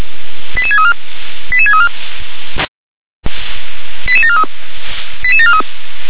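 Five-tone selective call (ZVEI), heard as quick runs of five tones, tones of about 70 ms. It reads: 88634 88634 88743 88743
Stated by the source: ZVEI five-tone call 88631 received on SDR.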